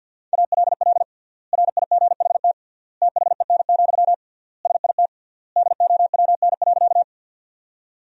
Morse code, heard as a keyed tone, about 47 wpm, 700 Hz. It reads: ALL RIGHT THEN, SIT DOWN.
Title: 3x Word-Spacing - Sentences from Top 500 Words - 50wpm